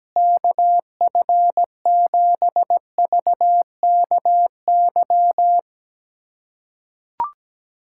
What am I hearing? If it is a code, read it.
Morse code, heard as a keyed tone, about 17 wpm, 700 Hz. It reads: KF7VKY
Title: USA Callsign DE Callsign 17wpm